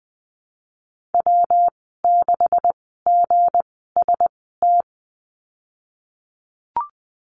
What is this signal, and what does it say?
Morse code, heard as a keyed tone, about 20 wpm, spaced as for 15 wpm, 700 Hz. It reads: W6GST
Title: 167 Lesson 27g - Number 7 - Call Signs 20wpm